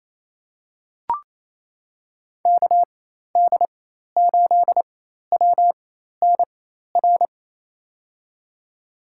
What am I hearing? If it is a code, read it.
Morse code, heard as a keyed tone, about 28 wpm, spaced as for 13 wpm, 700 Hz. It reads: KD8WNR
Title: USA Call Signs - 6 Characters - 4x Character Spacing - 28wpm